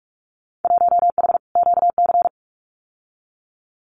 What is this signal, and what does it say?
Morse code, heard as a keyed tone, about 46 wpm, 700 Hz. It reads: 1H QC
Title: Winter Field Day Exchanges 45wpm